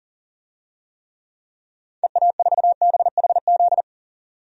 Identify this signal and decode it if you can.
Morse code, heard as a keyed tone, about 40 wpm, 700 Hz. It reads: EA4BHZ
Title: DXCC-100 Call Signs - 6 Characters - 40wpm